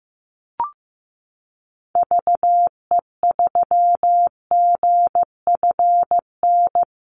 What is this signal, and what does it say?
Morse code, heard as a keyed tone, about 15 wpm, 700 Hz. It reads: VE3GFN
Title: Call Signs Repeated 2x - 15wpm